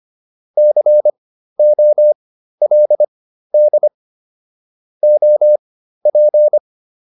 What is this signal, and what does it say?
Morse code, heard as a keyed tone, about 25 wpm, spaced as for 13 wpm, 600 Hz. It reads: COLD OP